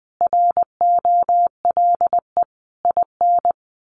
Morse code, heard as a keyed tone, about 20 wpm, 700 Hz. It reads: ROLE IN
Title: Sets of 5 Words Encore2 20wpm